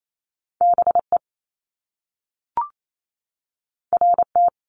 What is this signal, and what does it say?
Morse code, heard as a keyed tone, about 28 wpm, 700 Hz. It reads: BE RT
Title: Top Two Letter Combinations in Words 28wpm